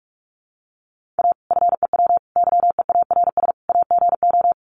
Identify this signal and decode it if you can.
Morse code, heard as a keyed tone, about 44 wpm, 700 Hz. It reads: A FEW YEARS AGO
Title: Sets of 4 Words 45wpm